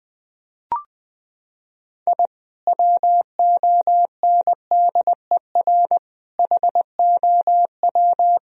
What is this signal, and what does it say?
Morse code, heard as a keyed tone, about 20 wpm, 700 Hz. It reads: I WONDER HOW
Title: Sentences from Top 1000 Words - Review Part 3 - 20wpm